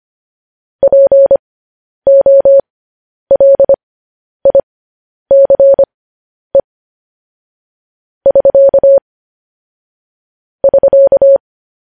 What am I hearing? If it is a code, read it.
Morse code, heard as a keyed tone, about 25 wpm, spaced as for 10 wpm, 550 Hz. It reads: POLICE <SK> <SK>
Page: news headlines in morse code at 10 wpm